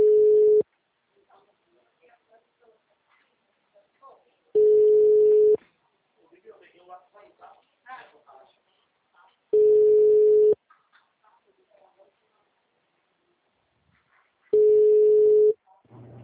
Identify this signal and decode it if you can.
Ringback tone (Europe, 425 Hz), with speech.